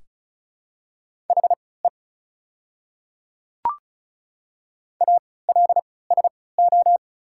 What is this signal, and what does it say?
Morse code, heard as a keyed tone, about 36 wpm, spaced as for 20 wpm, 700 Hz. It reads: HE ALSO
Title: 3x Character Spacing - Top 200 Words - 35wpm